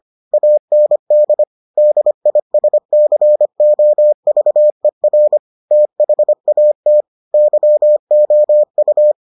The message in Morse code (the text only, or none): AND DISCOVER THAT YOU